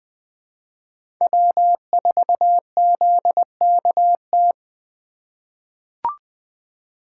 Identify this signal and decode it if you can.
Morse code, heard as a keyed tone, about 20 wpm, 700 Hz. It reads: W4ZKT
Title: USA Call Signs - 8x Character Spacing - 20wpm